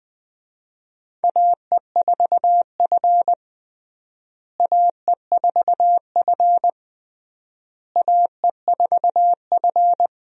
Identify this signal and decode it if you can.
Morse code, heard as a keyed tone, about 20 wpm, 700 Hz. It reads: AE4F AE4F AE4F
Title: USA Call Signs - 4 Characters - Repeated 3x - 20wpm